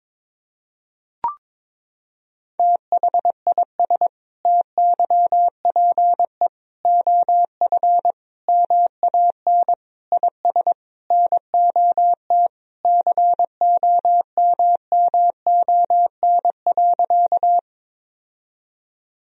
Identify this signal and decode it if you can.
Morse code, heard as a keyed tone, about 21 wpm, 700 Hz. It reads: THIS TYPE OF MAN IS NOT COMMON.
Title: Sentences from Top 1000 Words - Review Part 3 - 22wpm